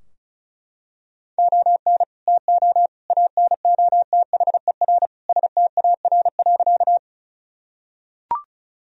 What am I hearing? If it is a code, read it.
Morse code, heard as a keyed tone, about 36 wpm, 700 Hz. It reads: ON TO ANOTHER STAR.
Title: Sentences from Top 1000 Words - Review Part 2 - 35wpm